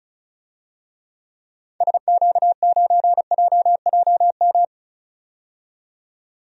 Morse code, heard as a keyed tone, about 36 wpm, 700 Hz. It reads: SQ9JJM
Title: DXCC-100 Call Signs - 6 Characters - 35wpm